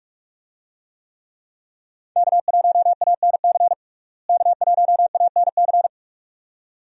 Morse code, heard as a keyed tone, about 42 wpm, 700 Hz. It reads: K1ANC K1ANC
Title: USA Call Signs - 5 Characters - Repeated 2x - 45wpm